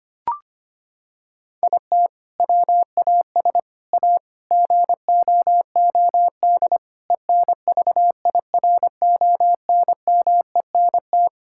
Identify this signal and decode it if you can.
Morse code, heard as a keyed tone, about 25 wpm, 700 Hz. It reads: IT WAS A GOOD ENVIRONMENT
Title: Sentences from Top 1000 Words - Review Part 5 - 25wpm